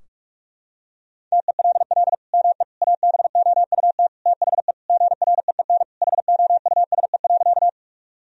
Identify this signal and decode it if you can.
Morse code, heard as a keyed tone, about 46 wpm, 700 Hz. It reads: TELL ME ABOUT THE GREEN HOUSE.
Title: Sentences from Top 1000 Words - Review Part 2 - 45wpm